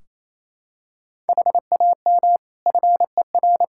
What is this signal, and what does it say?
Morse code, heard as a keyed tone, about 28 wpm, 700 Hz. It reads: HAM FER
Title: Practice QSO - Exchange 3 - More Info - 28wpm